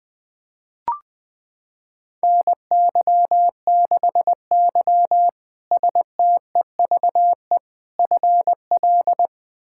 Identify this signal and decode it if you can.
Morse code, heard as a keyed tone, about 20 wpm, 700 Hz. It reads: NY6Y STEVE FL
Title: K1USN SST QSO Practice - 20wpm